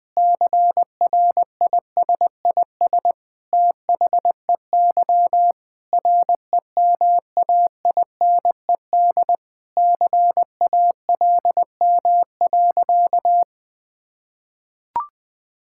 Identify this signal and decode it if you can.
Morse code, heard as a keyed tone, about 20 wpm, 700 Hz. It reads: CRISIS THEY REMAINED CALM.
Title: Sentences with 5-Word Phrases - 20wpm